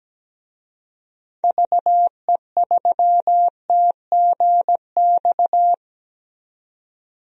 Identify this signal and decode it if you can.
Morse code, heard as a keyed tone, about 17 wpm, 700 Hz. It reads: VE3TGX
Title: DXCC-100 Call Signs - 6 Characters - 17wpm